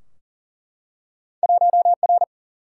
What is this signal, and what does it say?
Morse code, heard as a keyed tone, about 42 wpm, 700 Hz. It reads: 1R
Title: Single-Letter Single-Number 40wpm